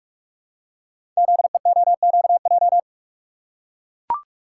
Morse code, heard as a keyed tone, about 44 wpm, 700 Hz. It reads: ZEOQJ
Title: Code Groups - 5 Letters - 45wpm